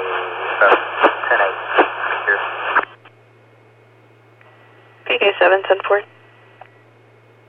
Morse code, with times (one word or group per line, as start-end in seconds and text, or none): none